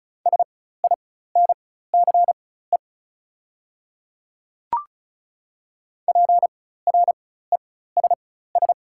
Morse code, heard as a keyed tone, about 34 wpm, spaced as for 16 wpm, 700 Hz. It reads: SINCE PRESS
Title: 4x Character Spacing - Top 300 Words - 35wpm